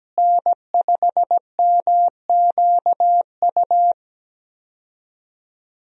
Morse code, heard as a keyed tone, about 17 wpm, 700 Hz. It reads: N5MQU
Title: USA Call Signs - 5 Characters - Repeated 2x - 17wpm